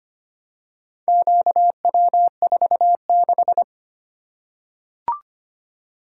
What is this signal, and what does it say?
Morse code, heard as a keyed tone, about 25 wpm, 700 Hz. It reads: QW46
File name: practice / 25wpm / Four Letter-Number Combinations - 25wpm